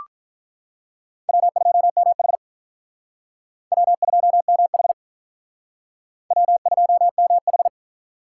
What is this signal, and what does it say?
Morse code, heard as a keyed tone, about 46 wpm, 700 Hz. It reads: W2MH W2MH W2MH